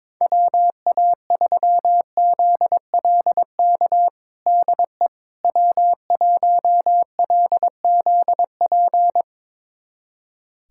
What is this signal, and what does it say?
Morse code, heard as a keyed tone, about 21 wpm, 700 Hz. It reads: WA3ZLK DE W1LZP
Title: USA Callsign DE Callsign 22wpm